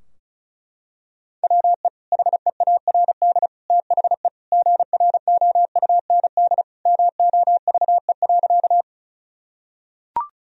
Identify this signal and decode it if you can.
Morse code, heard as a keyed tone, about 36 wpm, 700 Hz. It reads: WE HEARD THE GROUND MOVE.
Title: Sentences from Top 1000 Words - Review Part 2 - 35wpm